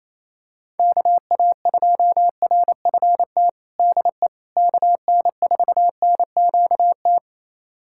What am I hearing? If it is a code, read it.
Morse code, heard as a keyed tone, about 28 wpm, 700 Hz. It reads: KA2RFT DE KN4NQT
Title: USA Callsign DE Callsign 28wpm